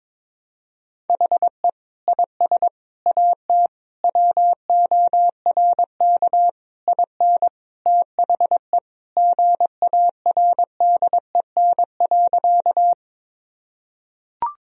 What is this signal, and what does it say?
Morse code, heard as a keyed tone, about 22 wpm, 700 Hz. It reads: HE IS AT WORK IN THE GARDEN.
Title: Sentences from Top 1000 Words - Review Part 4 - 22wpm